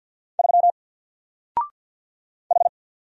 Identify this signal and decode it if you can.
Morse code, heard as a keyed tone, about 48 wpm, 700 Hz. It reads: 3 H
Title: Single Character (Letter:Number:Punctuation:Prosign) Alt 50wpm